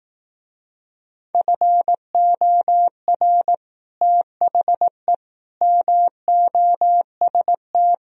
Morse code, heard as a keyed tone, about 18 wpm, 700 Hz. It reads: FOR THE MOST
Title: Speed-Racing - Sets of 4 Words 15wpm